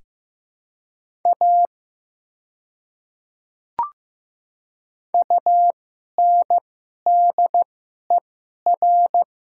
Morse code, heard as a keyed tone, about 15 wpm, spaced as for 11 wpm, 700 Hz. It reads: A UNDER
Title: 2x Character Spacing - Top 200 Words - 15wpm